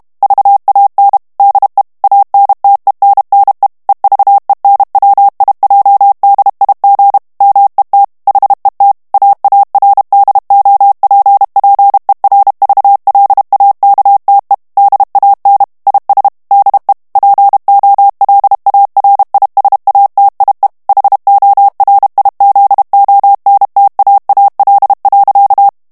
Morse code, heard as a keyed tone, about 32 wpm, 800 Hz. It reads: VAN DE ANTENNE EVENWIJDIG MET HET AARDOPPERVLAKTE DAN IS DE POLARISATIE HORIZONTAAL.
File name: HOLLANDAIS texte clair 1 vitesse 30 mots minute